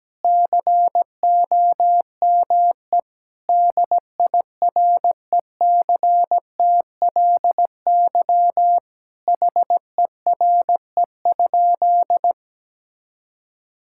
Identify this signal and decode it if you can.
Morse code, heard as a keyed tone, about 17 wpm, 700 Hz. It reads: COME DIRECTLY HERE?
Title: Sentences from Top 1000 Words - Review Part 5 - 17wpm